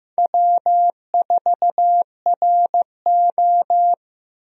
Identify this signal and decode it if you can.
Morse code, heard as a keyed tone, about 15 wpm, 700 Hz. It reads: W4RO
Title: USA Call Signs - 4 Characters - Repeated 2x - 15wpm